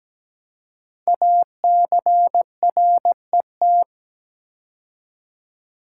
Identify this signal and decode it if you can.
Morse code, heard as a keyed tone, about 17 wpm, 700 Hz. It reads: ACRET
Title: Code Groups - 5 Letters - 17wpm